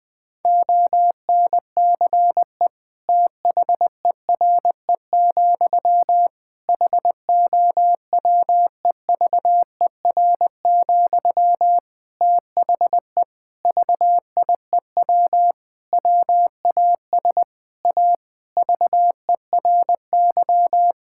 Morse code, heard as a keyed tone, about 21 wpm, 700 Hz. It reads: ONCE THERE, HOWEVER, THE VIEW WAS A VERY
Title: Sentences from Top 1000 Words - Review Part 6 - 20wpm